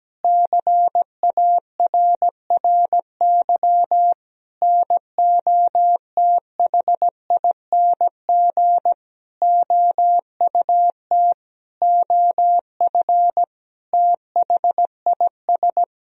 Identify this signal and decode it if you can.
Morse code, heard as a keyed tone, about 17 wpm, 700 Hz. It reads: CARRY NOTHING OUT OF THIS
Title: Sentences from Top 1000 Words - Review Part 1 - 17wpm